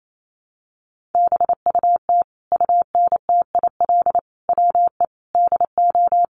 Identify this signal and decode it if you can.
Morse code, heard as a keyed tone, about 28 wpm, 700 Hz. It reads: BUT UNTIL WE DO
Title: A Sound Of Thunder 28wpm